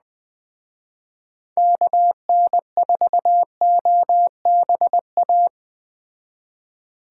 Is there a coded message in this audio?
Yes: Morse code from 1.6 to 5.5 s, heard as a keyed tone, about 20 wpm, 700 Hz. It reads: KN4OBA